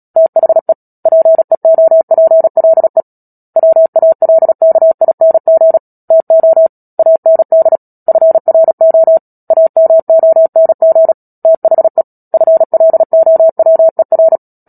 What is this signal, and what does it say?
Morse code, heard as a keyed tone, about 37 wpm, 650 Hz. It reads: THE PEOPLE WALKING TO AND FRO AMONG THE FLOWER
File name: Book Two / 10-Chapter 10 - Epilogue - 36 WPM